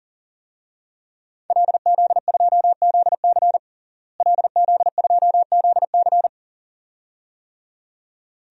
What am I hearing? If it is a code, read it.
Morse code, heard as a keyed tone, about 40 wpm, 700 Hz. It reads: LZ2ZC LZ2ZC
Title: DXCC-100 Call Signs - 5 Characters - Repeated 2x - 40wpm